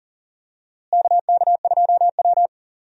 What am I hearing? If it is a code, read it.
Morse code, heard as a keyed tone, about 40 wpm, 700 Hz. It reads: KK2W
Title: USA Call Signs - 4 Characters - Repeated 2x - 40wpm